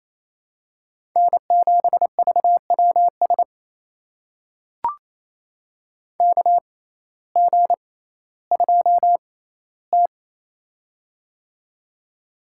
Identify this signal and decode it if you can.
Morse code, heard as a keyed tone, about 28 wpm, spaced as for 15 wpm, 700 Hz. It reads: N7VWS KG2T
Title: USA Call Signs - 6x Character Spacing - 28wpm